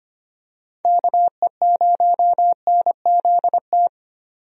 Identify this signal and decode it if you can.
Morse code, heard as a keyed tone, about 25 wpm, 700 Hz. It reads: KE0NZT
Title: USA Call Signs - 6 Characters - 25wpm